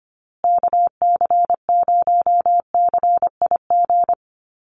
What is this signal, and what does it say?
Morse code, heard as a keyed tone, about 25 wpm, 700 Hz. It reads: KC0CIG
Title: USA Call Signs Repeated 2x 25wpm